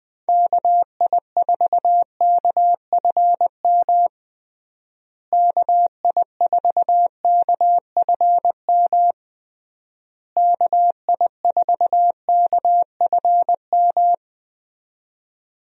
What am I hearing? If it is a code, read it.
Morse code, heard as a keyed tone, about 21 wpm, 700 Hz. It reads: KI4KFM KI4KFM KI4KFM
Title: USA Call Signs - 6 Characters - Repeated 3x - 20wpm